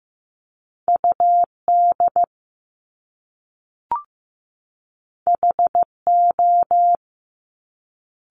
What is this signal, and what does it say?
Morse code, heard as a keyed tone, about 15 wpm, 700 Hz. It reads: UD HO
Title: Top Two Letter Combinations in Words 15wpm